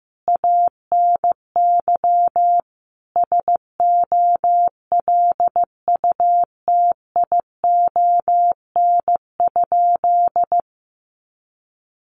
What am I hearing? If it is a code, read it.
Morse code, heard as a keyed tone, about 15 wpm, 700 Hz. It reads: ANY SOLUTION?
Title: Sentences from Top 1300 Words 15wpm